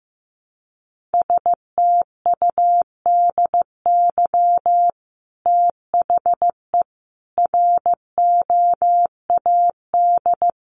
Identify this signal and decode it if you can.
Morse code, heard as a keyed tone, about 15 wpm, 700 Hz. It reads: STUDY THE ROAD
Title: Sentences from Top 1300 Words 15wpm